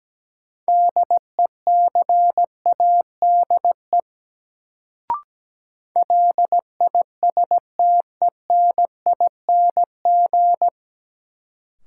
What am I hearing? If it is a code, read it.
Morse code, heard as a keyed tone, about 17 wpm, 700 Hz. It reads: DECADE LISTENING
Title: Sets of 5 Words - New Word Drill - 17wpm